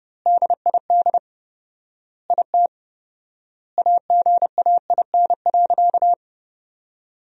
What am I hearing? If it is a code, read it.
Morse code, heard as a keyed tone, about 30 wpm, 700 Hz. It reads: DID IT AGAIN.
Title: 4x Word-Spacing - Sentences from Top 300 Words - 30wpm